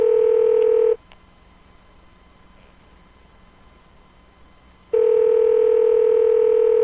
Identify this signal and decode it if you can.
Ringback tone (North America), 440 + 480 Hz.